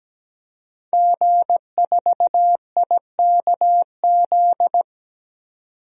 Morse code, heard as a keyed tone, about 17 wpm, 700 Hz. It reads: G4IKZ
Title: DXCC-100 Call Signs - 5 Characters - 17wpm